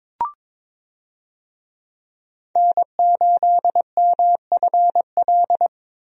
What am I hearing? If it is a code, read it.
Morse code, heard as a keyed tone, about 22 wpm, 700 Hz. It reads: N8MFL
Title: USA Call Signs - 5 Characters - 22wpm